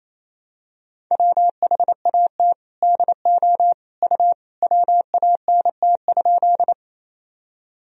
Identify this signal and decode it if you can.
Morse code, heard as a keyed tone, about 28 wpm, 700 Hz. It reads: WHAT DO U WANT?
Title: Sentences from Top 200 Words and Abbreviations 28wpm